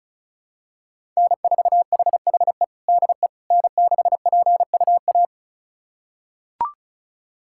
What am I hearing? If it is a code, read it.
Morse code, heard as a keyed tone, about 36 wpm, 700 Hz. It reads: N4HHE DE N6PUA
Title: USA Callsign DE Callsign - 5 Characters - 35wpm